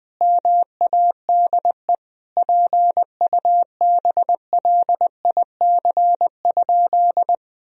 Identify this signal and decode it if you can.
Morse code, heard as a keyed tone, about 21 wpm, 700 Hz. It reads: MADE PUBLIC?
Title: Sentences from Top 1000 Words - Review Part 3 - 20wpm